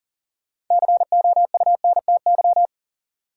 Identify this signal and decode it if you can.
Morse code, heard as a keyed tone, about 40 wpm, 700 Hz. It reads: COUNTY
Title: Sentences from Top 1200 Words - New Word Drill Alt - 40wpm